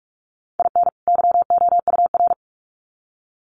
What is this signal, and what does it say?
Morse code, heard as a keyed tone, about 44 wpm, 700 Hz. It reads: IN YOUR
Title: Sets of 2 Words 45wpm